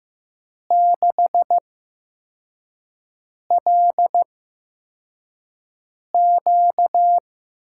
Morse code, heard as a keyed tone, about 15 wpm, spaced as for 4 wpm, 700 Hz. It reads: 6LQ
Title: USA Call Signs - 5 Characters - 8x Character Spacing - 15wpm